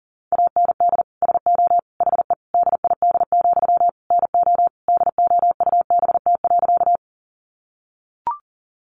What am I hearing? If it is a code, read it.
Morse code, heard as a keyed tone, about 37 wpm, 700 Hz. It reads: AND SO HE DID, NO DOUBT.
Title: Sentences from Top 1500 Words 40wpm